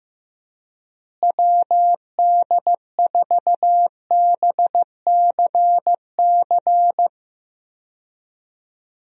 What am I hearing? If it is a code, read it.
Morse code, heard as a keyed tone, about 15 wpm, 700 Hz. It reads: WD4BCC
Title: USA Call Signs - 6 Characters - Repeated 3x - 15wpm